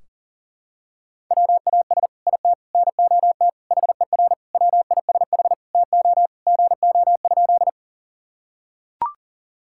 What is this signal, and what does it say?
Morse code, heard as a keyed tone, about 38 wpm, 700 Hz. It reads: WAS IT NOT HER WISH TO GO?
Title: Sentences from Top 1600 Words 40wpm